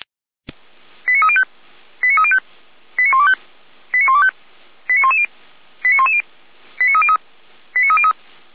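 Five-tone selective call (ZVEI), heard as quick runs of five tones, tones of about 70 ms. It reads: 89305 89305 89125 89125 89110 89110 89303 89303
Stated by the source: ZVEI selective call received on an SDR.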